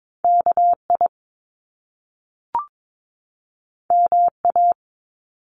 Morse code, heard as a keyed tone, about 22 wpm, 700 Hz. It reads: KI MA
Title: Top Two Letter Combinations in Words 22wpm